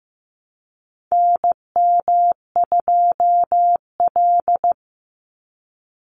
Morse code, heard as a keyed tone, about 15 wpm, 700 Hz. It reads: NM2L
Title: Call Signs Repeated 3x - 15wpm